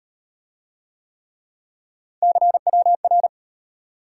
Morse code, heard as a keyed tone, about 36 wpm, 700 Hz. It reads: CWR